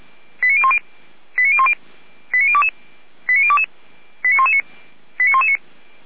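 Five-tone selective call (ZVEI), heard as quick runs of five tones, tones of about 70 ms. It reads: 89010 89010 89022 89022 89119 89119
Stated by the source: ZVEI five-tone selective call received with an SDR.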